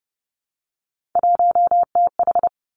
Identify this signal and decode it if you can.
Morse code, heard as a keyed tone, about 30 wpm, 700 Hz. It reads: 1TH
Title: Three Challenging Letter-Number Combinations 30wpm